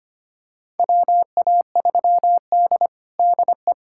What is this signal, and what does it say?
Morse code, heard as a keyed tone, about 25 wpm, 700 Hz. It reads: WA3D DE
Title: USA Callsign DE Callsign - 4 Characters - 25wpm